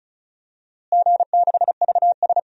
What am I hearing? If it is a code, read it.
Morse code, heard as a keyed tone, about 36 wpm, 690 Hz. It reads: G6VS
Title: DXCC-100 Call Signs - 4 Characters - Repeated 2x - 35wpm